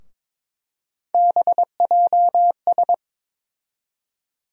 Morse code, heard as a keyed tone, about 22 wpm, 700 Hz. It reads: BJS
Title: USA Call Sign Postfixes - 3 Letters 22wpm